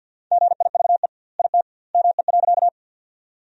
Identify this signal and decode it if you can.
Morse code, heard as a keyed tone, about 52 wpm, 700 Hz. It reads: GIVE IT ME.
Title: 2x Word-Spacing - Sentences from Top 200 Words - 50wpm